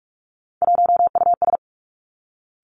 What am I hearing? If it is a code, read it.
Morse code, heard as a keyed tone, about 46 wpm, 700 Hz. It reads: 1US
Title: Three Challenging Letter-Number Combinations 45wpm